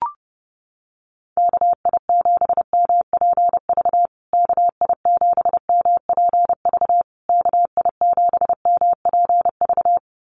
Morse code, heard as a keyed tone, about 30 wpm, 700 Hz. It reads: KI7MPV KI7MPV KI7MPV
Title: USA Call Signs Repeated 3x 30wpm